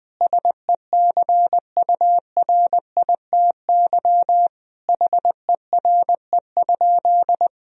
Morse code, heard as a keyed tone, about 20 wpm, 700 Hz. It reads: SECURITY HERE?
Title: Sentences from Top 1000 Words - Review Part 4 - 20wpm